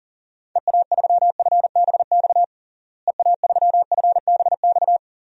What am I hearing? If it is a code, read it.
Morse code, heard as a keyed tone, about 40 wpm, 700 Hz. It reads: EA3FBX EA3FBX
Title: DXCC-100 Call Signs - 6 Characters - Repeated 2x - 40wpm